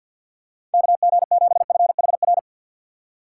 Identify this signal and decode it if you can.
Morse code, heard as a keyed tone, about 49 wpm, 700 Hz. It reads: KG7FHR
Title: USA Call Signs - 6x Character Spacing - 50wpm